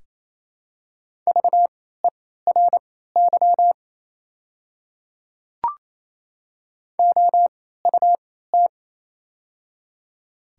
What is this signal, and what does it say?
Morse code, heard as a keyed tone, about 27 wpm, spaced as for 16 wpm, 690 Hz. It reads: VERY OUT
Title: 3x Character Spacing - Top 200 Words - 28wpm